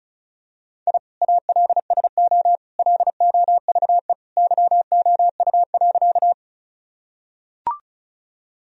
Morse code, heard as a keyed tone, about 36 wpm, 700 Hz. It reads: I ALSO LOVE YOU.
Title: Sentences from Top 1000 Words - Review Part 1 - 35wpm